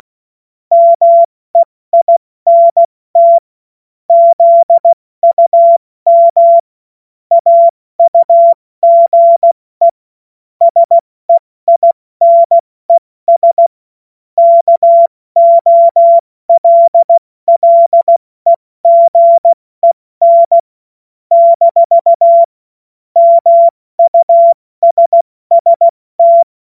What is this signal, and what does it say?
Morse code, heard as a keyed tone, about 16 wpm, 680 Hz. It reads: MEINT ZUM AUGE SEINES KOLLEGEN - MUSST
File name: Morsekurs / Witze / Witz 19 Dezember 2023 70 F80